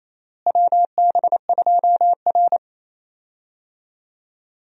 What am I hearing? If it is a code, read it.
Morse code, heard as a keyed tone, about 28 wpm, 700 Hz. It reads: WB2R
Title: USA Call Signs - 4 Characters - Repeated 2x - 28wpm